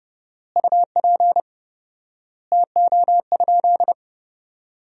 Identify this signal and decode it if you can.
Morse code, heard as a keyed tone, about 30 wpm, 700 Hz. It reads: UP TO?
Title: 4x Word-Spacing - Sentences from Top 500 Words - 30wpm